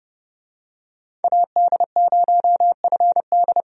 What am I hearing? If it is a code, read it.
Morse code, heard as a keyed tone, about 30 wpm, 700 Hz. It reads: AD0FD
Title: USA Call Signs - 5 Characters - 30wpm